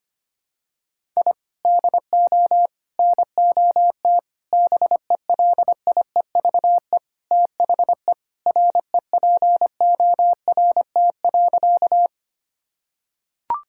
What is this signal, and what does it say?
Morse code, heard as a keyed tone, about 25 wpm, 700 Hz. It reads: I DO NOT BELIEVE THE REPORT.
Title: Sentences from Top 1000 Words - Review Part 6 - 25wpm